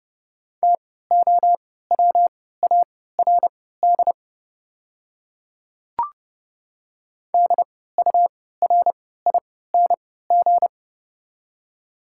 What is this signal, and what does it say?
Morse code, heard as a keyed tone, about 32 wpm, spaced as for 17 wpm, 690 Hz. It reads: TOWARD DURING
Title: 3x Character Spacing - Top 500 Words - 30wpm